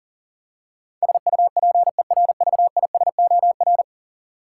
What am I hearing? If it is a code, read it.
Morse code, heard as a keyed tone, about 40 wpm, 700 Hz. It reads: SUPERVISOR
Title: Practice QSO - Exchange 3 - More Info - 40wpm